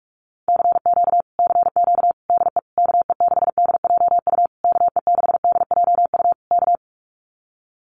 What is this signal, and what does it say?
Morse code, heard as a keyed tone, about 44 wpm, 700 Hz. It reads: CQ CQ DE KE6DJU KE6DJU K
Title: Common QSO Phrases Encore 45wpm